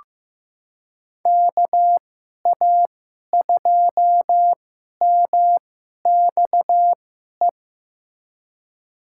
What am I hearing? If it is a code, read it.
Morse code, heard as a keyed tone, about 15 wpm, spaced as for 11 wpm, 700 Hz. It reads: KA2MXE
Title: USA Call Signs - 6 Characters - 2x Character Spacing - 15wpm